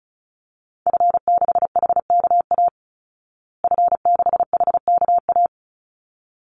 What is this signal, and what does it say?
Morse code, heard as a keyed tone, about 34 wpm, 700 Hz. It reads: F6HKA F6HKA
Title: Call Signs Repeated 2x - 35wpm